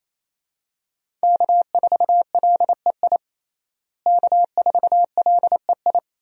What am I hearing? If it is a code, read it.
Morse code, heard as a keyed tone, about 28 wpm, 700 Hz. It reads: K4LEI K4LEI
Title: USA Call Signs - 5 Characters - Repeated 3x - 28wpm